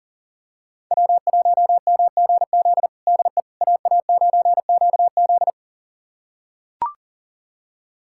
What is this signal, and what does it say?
Morse code, heard as a keyed tone, about 37 wpm, 700 Hz. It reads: W1MGZ DE AA9QZ
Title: USA Callsign DE Callsign - 5 Characters - 40wpm